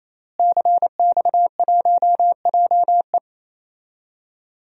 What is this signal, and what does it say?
Morse code, heard as a keyed tone, about 28 wpm, 700 Hz. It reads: CX1JE
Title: DXCC-100 Call Signs - 5 Characters - Repeated 3x - 28wpm